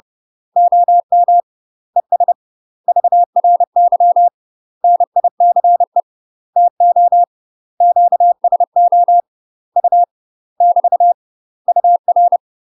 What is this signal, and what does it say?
Morse code, heard as a keyed tone, about 30 wpm, 700 Hz. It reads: OM ES VRY NICE TO QSO U <BT> UR
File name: UK MASTER QSOs TRAINING / 30WPM / UK MASTER QSOs 0041